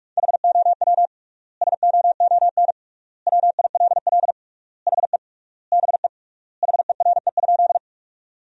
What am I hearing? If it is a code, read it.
Morse code, heard as a keyed tone, about 46 wpm, 700 Hz. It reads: HOW SOON WILL HE BE HERE?
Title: 3x Word-Spacing - Sentences from Top 400 Words - 45wpm